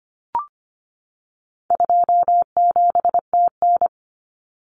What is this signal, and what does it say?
Morse code, heard as a keyed tone, about 25 wpm, 700 Hz. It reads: 27TN
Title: Common Cut Numbers - 4-Digit 25wpm